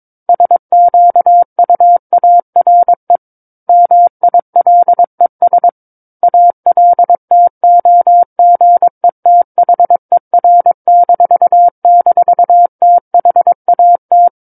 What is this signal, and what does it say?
Morse code, heard as a keyed tone, about 22 wpm, 700 Hz. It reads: SQUARE MILES ALTOGETHER--THAT
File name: Book One / 13-Chapter 13 - 22 WPM